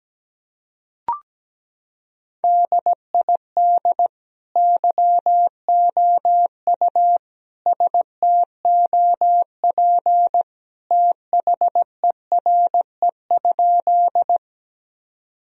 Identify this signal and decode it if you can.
Morse code, heard as a keyed tone, about 17 wpm, 700 Hz. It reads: DID YOU STOP THERE?
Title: Sentences from Top 1000 Words - Review Part 1 - 17wpm